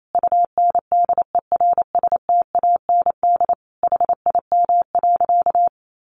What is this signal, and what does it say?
Morse code, heard as a keyed tone, about 28 wpm, 700 Hz. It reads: UNDERSTAND HIM.
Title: Sentences from Top 600 Words 28wpm